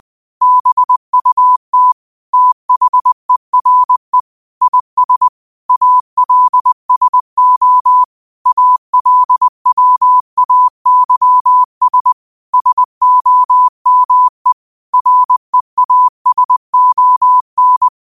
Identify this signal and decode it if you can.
Morse code, heard as a keyed tone, about 20 wpm, 1000 Hz. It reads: BUT THERE IS ALSO ALWAYS SOME REASON
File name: qotdfiles / 20wpm